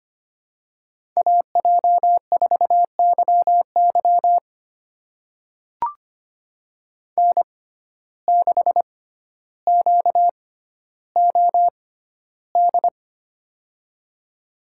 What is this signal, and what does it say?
Morse code, heard as a keyed tone, about 25 wpm, spaced as for 13 wpm, 700 Hz. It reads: AJ4YY N6QOD
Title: USA Call Signs - 5 Characters - 6x Character Spacing - 25wpm